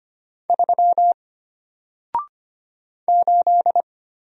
Morse code, heard as a keyed tone, about 25 wpm, 700 Hz. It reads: 3 8
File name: Single-Letter-Number-25wpm